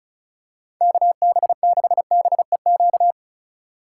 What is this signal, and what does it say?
Morse code, heard as a keyed tone, about 34 wpm, 700 Hz. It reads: KB6BEQ
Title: USA Call Signs - 2x Character Spacing - 35wpm